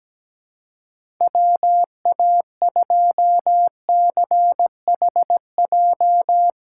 Morse code, heard as a keyed tone, about 17 wpm, 700 Hz. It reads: WA2CHJ
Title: USA Call Signs - 6 Characters - 3x Character Spacing - 17wpm